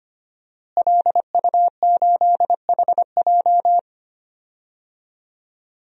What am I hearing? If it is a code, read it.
Morse code, heard as a keyed tone, about 25 wpm, 700 Hz. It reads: LU8HJ
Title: DXCC-100 Call Signs - 5 Characters - Repeated 3x - 25wpm